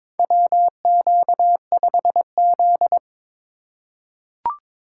Morse code, heard as a keyed tone, about 22 wpm, 700 Hz. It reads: WQ5Z
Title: USA Call Signs - 4 Characters - 3x Character Spacing - 22wpm